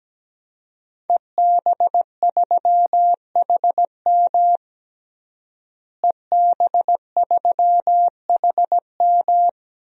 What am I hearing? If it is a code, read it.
Morse code, heard as a keyed tone, about 17 wpm, 700 Hz. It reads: EB3HM EB3HM
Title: DXCC-100 Call Signs - 5 Characters - Repeated 2x - 17wpm